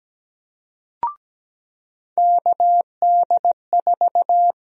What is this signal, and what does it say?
Morse code, heard as a keyed tone, about 17 wpm, 700 Hz. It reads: KD4
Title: USA Call Sign Prefixes - 17wpm